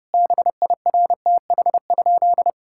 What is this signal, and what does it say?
Morse code, heard as a keyed tone, about 30 wpm, 700 Hz. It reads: BIRTH?
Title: Sentences from Top 1000 Words - Review Part 5 - 30wpm